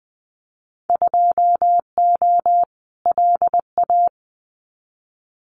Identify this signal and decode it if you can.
Morse code, heard as a keyed tone, about 20 wpm, 700 Hz. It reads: 2O LA
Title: Winter Field Day Exchanges 20wpm